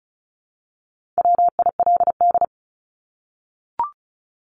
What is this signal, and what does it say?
Morse code, heard as a keyed tone, about 34 wpm, 700 Hz. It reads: WILD
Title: Top 1501-1750 Words 35wpm